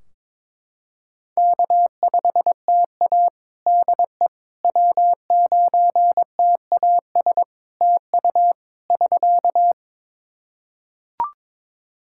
Morse code, heard as a keyed tone, about 22 wpm, 700 Hz. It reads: K5TA DE W9TAS TU <SK>
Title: Practice QSO - Exchange 4 - Ending - 22wpm